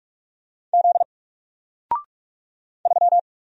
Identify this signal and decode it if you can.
Morse code, heard as a keyed tone, about 46 wpm, 700 Hz. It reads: Z 3
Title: Single Letter-Number Alt 45wpm